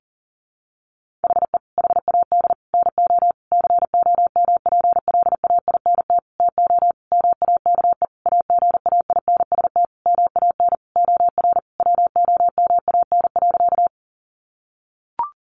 Morse code, heard as a keyed tone, about 40 wpm, 700 Hz. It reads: HE HAD NO COMPLAINT TO MAKE AGAINST MAN OR WOMAN.
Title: Sentences from Top 1400 Words 40wpm